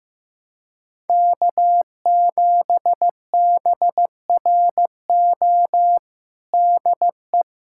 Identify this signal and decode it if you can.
Morse code, heard as a keyed tone, about 15 wpm, 700 Hz. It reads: K7BRO DE
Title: USA Callsign DE Callsign - 5 Characters - 15wpm